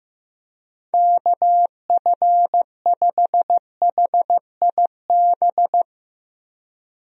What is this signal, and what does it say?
Morse code, heard as a keyed tone, about 15 wpm, 700 Hz. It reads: KF5HIB
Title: USA Call Signs - 6 Characters - 15wpm